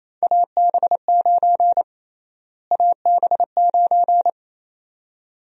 Morse code, heard as a keyed tone, about 28 wpm, 700 Hz. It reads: AB9 AB9
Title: USA Call Sign Prefixes - Repeated 2x - 28wpm